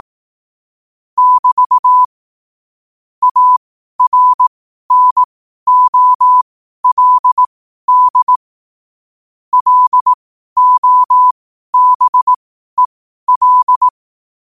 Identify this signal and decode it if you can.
Morse code, heard as a keyed tone, about 18 wpm, spaced as for 12 wpm, 1000 Hz. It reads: <BT> ARNOLD LOBEL
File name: qotdfiles / 12wpm